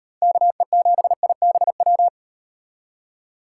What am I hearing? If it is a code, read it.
Morse code, heard as a keyed tone, about 38 wpm, 700 Hz. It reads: KE7IBW